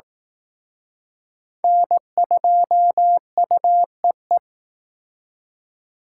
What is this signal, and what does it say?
Morse code, heard as a keyed tone, about 19 wpm, 700 Hz. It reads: N2UEE